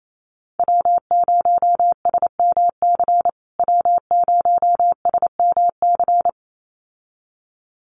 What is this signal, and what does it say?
Morse code, heard as a keyed tone, about 28 wpm, 700 Hz. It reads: W0SMC W0SMC
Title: USA Call Signs Repeated 2x 28wpm